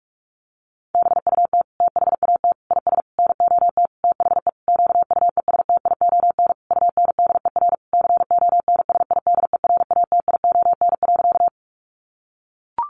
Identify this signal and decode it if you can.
Morse code, heard as a keyed tone, about 44 wpm, 700 Hz. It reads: BUT THAT IS NOT THE QUESTION UNDER CONSIDERATION.
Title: Sentences from Top 1500 Words 45wpm